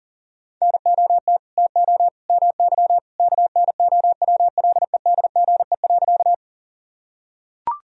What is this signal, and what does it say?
Morse code, heard as a keyed tone, about 37 wpm, 700 Hz. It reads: NOT TO MY KNOWLEDGE.
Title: Sentences from Top 1000 Words - Review Part 3 - 40wpm